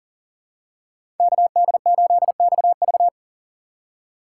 Morse code, heard as a keyed tone, about 40 wpm, 700 Hz. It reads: KD8XV
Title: USA Call Signs - 5 Characters - 4x Character Spacing - 40wpm